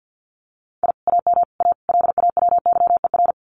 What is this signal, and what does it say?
Morse code, heard as a keyed tone, about 50 wpm, 700 Hz. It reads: I AM A LAWYER
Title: Common QSO Phrases Encore 50wpm